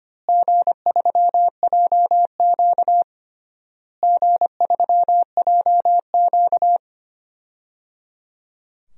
Morse code, heard as a keyed tone, about 25 wpm, 700 Hz. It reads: G3JQ G3JQ
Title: DXCC-100 Call Signs - 4 Characters - Repeated 3x - 25wpm